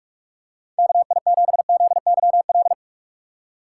Morse code, heard as a keyed tone, about 46 wpm, 700 Hz. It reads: KI7ZYL
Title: USA Call Signs - 2x Character Spacing - 45wpm